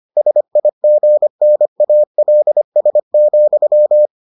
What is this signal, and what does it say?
Morse code, heard as a keyed tone, about 25 wpm, 600 Hz. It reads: SIGNALS,